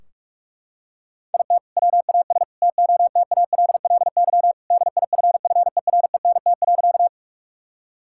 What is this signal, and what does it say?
Morse code, heard as a keyed tone, about 41 wpm, 700 Hz. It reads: IT WAS TOTALLY DIFFERENT.
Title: Sentences from Top 1600 Words 45wpm